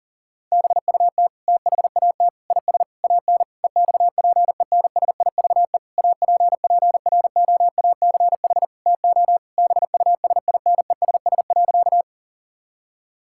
Morse code, heard as a keyed tone, about 40 wpm, 700 Hz. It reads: BUT THAT IS AN EXPENSIVE APPROACH TO BUSINESS.
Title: Sentences from Top 1000 Words - Review Part 6 - 40wpm